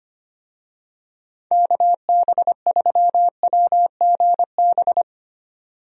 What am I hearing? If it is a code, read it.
Morse code, heard as a keyed tone, about 25 wpm, 700 Hz. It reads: KB3WGB
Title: USA Call Signs - 6 Characters - 25wpm